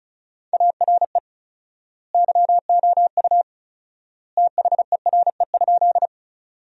Morse code, heard as a keyed tone, about 34 wpm, 700 Hz. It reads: ARE YOU THERE?
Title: 4x Word-Spacing - Sentences from Top 100 Words - 35wpm